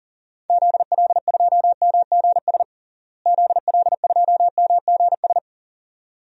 Morse code, heard as a keyed tone, about 40 wpm, 700 Hz. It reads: ZL2MGS ZL2MGS
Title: DXCC-100 Call Signs - 6 Characters - Repeated 2x - 40wpm